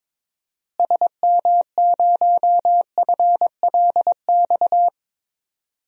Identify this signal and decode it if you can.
Morse code, heard as a keyed tone, about 22 wpm, 700 Hz. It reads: SM0FLX
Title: DXCC-100 Call Signs - 6 Characters - Repeated 3x - 22wpm